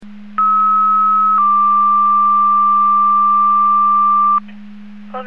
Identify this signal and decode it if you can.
Two-tone page, heard as a one-second tone, then a three-second tone, about 1290 then 1180 Hz. It is heard from 0.4 to 4.4 s.
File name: Portage 154.385 Paging